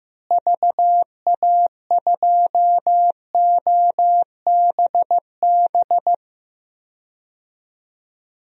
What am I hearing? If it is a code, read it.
Morse code, heard as a keyed tone, about 15 wpm, 700 Hz. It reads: VA2OBB
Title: DXCC-100 Call Signs - 6 Characters - Repeated 3x - 15wpm